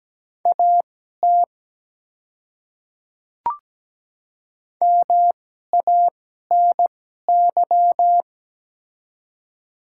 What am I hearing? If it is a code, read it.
Morse code, heard as a keyed tone, about 17 wpm, spaced as for 12 wpm, 700 Hz. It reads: AT MANY
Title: 2x Character Spacing - Top 100 Words - 17wpm